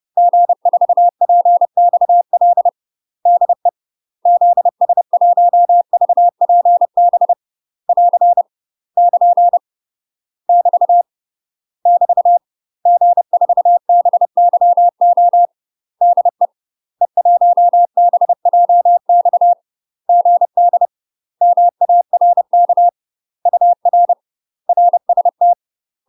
Morse code, heard as a keyed tone, about 30 wpm, 700 Hz. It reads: G4PXL DE ZS1VPB <AR> <KN> <BT> <BT> G4BYO DE E1BJX GD MARK UR RST
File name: UK MASTER QSOs TRAINING / 30WPM / UK MASTER QSOs 0025